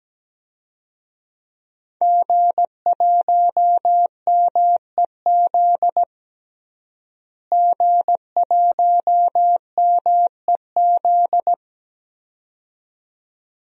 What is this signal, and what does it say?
Morse code, heard as a keyed tone, about 17 wpm, 700 Hz. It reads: G1MEZ G1MEZ
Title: DXCC-100 Call Signs - 5 Characters - Repeated 2x - 17wpm